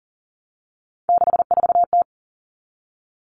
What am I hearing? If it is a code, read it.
Morse code, heard as a keyed tone, about 37 wpm, 700 Hz. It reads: 64T
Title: Three Challenging Letter-Number Combinations 40wpm